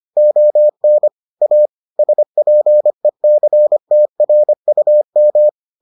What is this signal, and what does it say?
Morse code, heard as a keyed tone, about 25 wpm, 600 Hz. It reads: ON A SPECTRUM